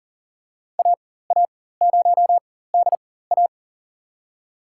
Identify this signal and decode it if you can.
Morse code, heard as a keyed tone, about 40 wpm, spaced as for 19 wpm, 700 Hz. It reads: AA0DA
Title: USA Call Signs - 5 Characters - 4x Character Spacing - 40wpm